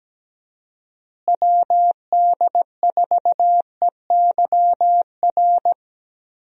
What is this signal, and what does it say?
Morse code, heard as a keyed tone, about 17 wpm, 700 Hz. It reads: WD4EYR
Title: USA Call Signs - 6 Characters - Repeated 3x - 17wpm